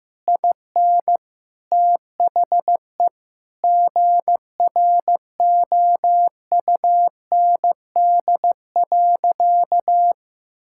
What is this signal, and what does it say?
Morse code, heard as a keyed tone, about 15 wpm, 700 Hz. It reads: IN THE GROUND.
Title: Sentences from Top 1600 Words 15wpm